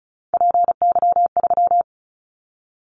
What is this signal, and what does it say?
Morse code, heard as a keyed tone, about 36 wpm, 700 Hz. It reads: PY3
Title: Callsign Prefixes for Top 50 DXCC - 35wpm